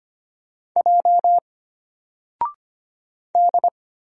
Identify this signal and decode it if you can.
Morse code, heard as a keyed tone, about 25 wpm, 700 Hz. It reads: J D
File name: Single-Letter-Number-25wpm